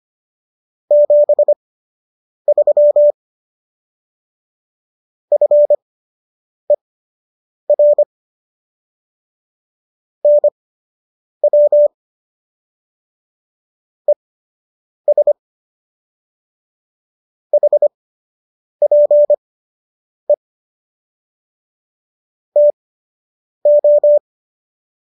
Morse code, heard as a keyed tone, about 25 wpm, spaced as for 8 wpm, 590 Hz. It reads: 73 FER NW ES HPE TO